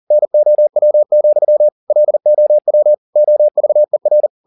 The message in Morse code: NOW, LOW OVER

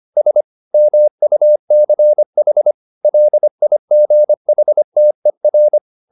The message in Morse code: S MUCH LIGHTER

